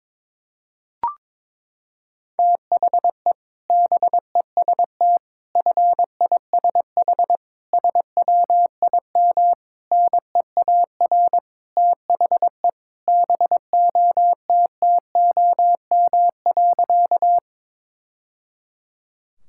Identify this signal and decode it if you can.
Morse code, heard as a keyed tone, about 22 wpm, 700 Hz. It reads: THE BEST FISH SWIM NEAR THE BOTTOM.
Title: Sentences from Top 1000 Words - Review Part 6 - 22wpm